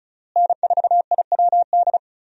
Morse code, heard as a keyed tone, about 34 wpm, 700 Hz. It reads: N4IWD
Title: USA Call Signs - 5 Characters - Repeated 3x - 35wpm